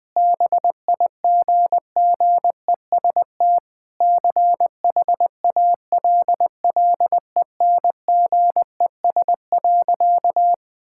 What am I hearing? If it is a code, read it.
Morse code, heard as a keyed tone, about 20 wpm, 700 Hz. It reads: BIGGEST CHALLENGES.
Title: Sentences with 4-Word Phrases - 20wpm